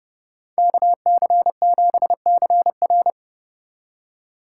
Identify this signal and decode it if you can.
Morse code, heard as a keyed tone, about 29 wpm, 700 Hz. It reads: KC7CR